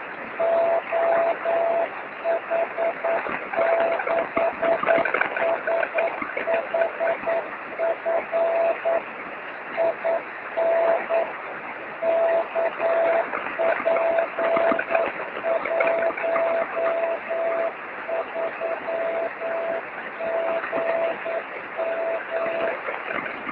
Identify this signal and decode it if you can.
Morse code, heard as a keyed tone, about 9 wpm, 660 Hz, with noise. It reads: OH6SHF IN KP13GM
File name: sm7lcb / oh6shf 23cm